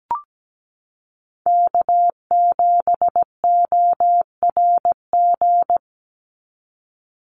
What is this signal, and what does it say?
Morse code, heard as a keyed tone, about 17 wpm, 700 Hz. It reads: K7ORG
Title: NA SOTA Chaser Callsigns 17wpm